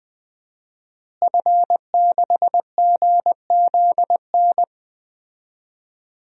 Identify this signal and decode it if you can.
Morse code, heard as a keyed tone, about 20 wpm, 700 Hz. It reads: F6GZN
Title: DXCC-100 Call Signs - 5 Characters - Repeated 3x - 20wpm